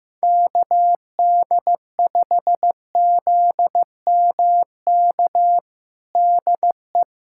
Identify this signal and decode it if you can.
Morse code, heard as a keyed tone, about 15 wpm, 700 Hz. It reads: KD5ZMK DE
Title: USA Callsign DE Callsign - 6 Characters - 15wpm